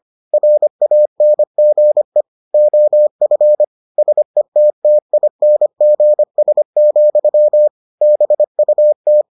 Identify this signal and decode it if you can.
Morse code, heard as a keyed tone, about 25 wpm, 600 Hz. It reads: RANGE OF SETTINGS, BUT